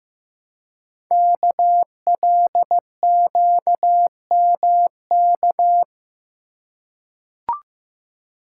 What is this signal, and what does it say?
Morse code, heard as a keyed tone, about 15 wpm, 700 Hz. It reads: KLQMK
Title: Code Groups - 5 Letters - 15wpm